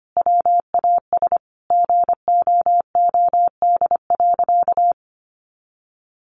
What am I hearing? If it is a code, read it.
Morse code, heard as a keyed tone, about 25 wpm, 700 Hz. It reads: WAS GOOD.
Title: Sentences from Top 1300 Words 25wpm